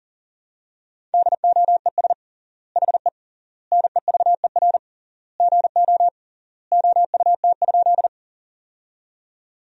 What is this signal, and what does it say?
Morse code, heard as a keyed tone, about 40 wpm, 700 Hz. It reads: DOES HE NEVER GO OUT?
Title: 3x Word-Spacing - Sentences from Top 300 Words - 40wpm